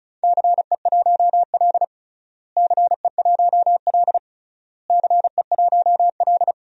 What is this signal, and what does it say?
Morse code, heard as a keyed tone, about 34 wpm, 700 Hz. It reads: CE1L CE1L CE1L
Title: DXCC-100 Call Signs - 4 Characters - Repeated 3x - 35wpm